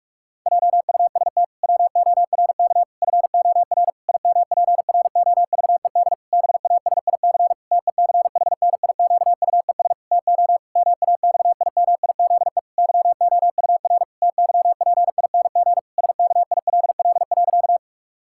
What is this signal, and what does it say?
Morse code, heard as a keyed tone, about 46 wpm, 700 Hz. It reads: JUST WORK FOR IMPROVED BASIC TECHNIQUES TO MAXIMIZE YOUR TYPING SKILL.
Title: Pangram Challenge v2 - 45wpm